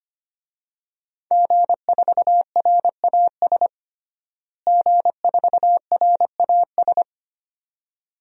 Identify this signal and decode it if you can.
Morse code, heard as a keyed tone, about 25 wpm, 700 Hz. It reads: G4RAS G4RAS
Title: DXCC-100 Call Signs - 5 Characters - Repeated 2x - 25wpm